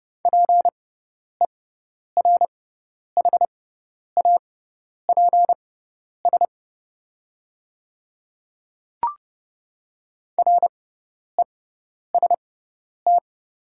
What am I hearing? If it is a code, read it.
Morse code, heard as a keyed tone, about 30 wpm, spaced as for 10 wpm, 700 Hz. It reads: PERHAPS REST
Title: 6x Character Spacing - Top 500 Words - 30wpm